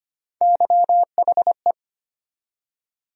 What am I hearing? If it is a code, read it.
Morse code, heard as a keyed tone, about 25 wpm, 700 Hz. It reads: YHE
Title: USA Call Sign Postfixes - 3 Letters 25wpm